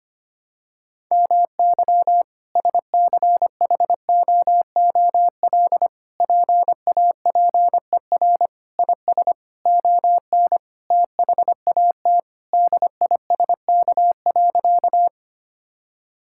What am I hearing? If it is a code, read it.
Morse code, heard as a keyed tone, about 25 wpm, 700 Hz. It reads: MY SCHOOL PAPER IS ON THAT DISK.
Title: Sentences from Top 1000 Words - Review Part 6 - 25wpm